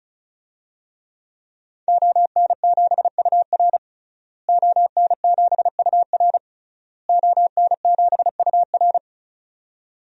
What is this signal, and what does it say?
Morse code, heard as a keyed tone, about 34 wpm, 700 Hz. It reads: ON7UR ON7UR ON7UR
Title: DXCC-100 Call Signs - 5 Characters - Repeated 3x - 35wpm